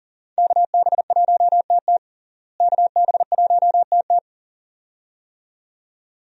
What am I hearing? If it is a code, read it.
Morse code, heard as a keyed tone, about 37 wpm, 700 Hz. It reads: KB1TT KB1TT
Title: USA Call Signs - 5 Characters - Repeated 2x - 40wpm